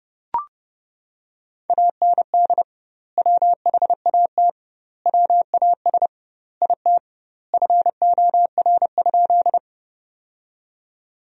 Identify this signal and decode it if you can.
Morse code, heard as a keyed tone, about 30 wpm, 700 Hz. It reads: AND WHAT WAS IT FOR?
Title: 2x Word-Spacing - Sentences from Top 100 Words - 30wpm